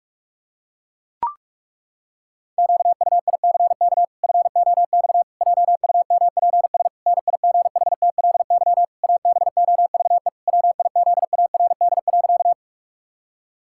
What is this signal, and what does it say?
Morse code, heard as a keyed tone, about 43 wpm, 700 Hz. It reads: QUICK FOX JUMPS NIGHTLY ABOVE WIZARD.
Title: Pangram Challenge v2 - 45wpm